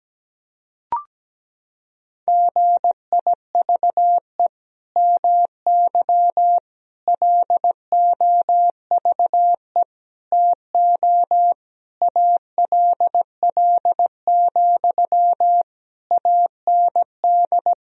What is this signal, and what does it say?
Morse code, heard as a keyed tone, about 17 wpm, 700 Hz. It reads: GIVE MY LOVE TO ALL, AND
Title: Sentences from Top 1000 Words - Review Part 4 - 17wpm